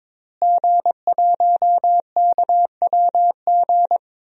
Morse code, heard as a keyed tone, about 21 wpm, 700 Hz. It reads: G1KWG
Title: DXCC-100 Call Signs - 5 Characters - Repeated 2x - 22wpm